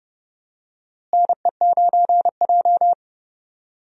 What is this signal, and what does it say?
Morse code, heard as a keyed tone, about 30 wpm, 700 Hz. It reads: NE9J
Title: USA Call Signs - 4 Characters - Repeated 3x - 30wpm